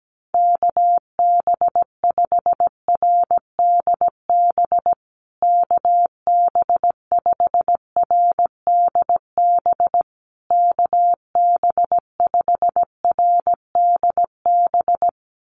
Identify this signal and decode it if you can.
Morse code, heard as a keyed tone, about 17 wpm, 700 Hz. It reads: KB5RDB KB5RDB KB5RDB
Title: USA Call Signs Repeated 3x 17wpm